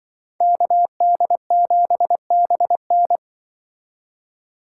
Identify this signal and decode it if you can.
Morse code, heard as a keyed tone, about 23 wpm, 700 Hz. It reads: KD7BN
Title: Speed Racing - USA 2x2 Call Signs 20wpm